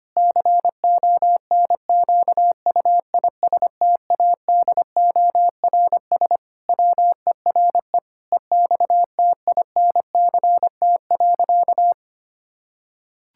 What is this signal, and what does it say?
Morse code, heard as a keyed tone, about 25 wpm, 700 Hz. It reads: CONQUISTADORS WERE EXTINCT.
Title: Pangram Challenge v2 - 25wpm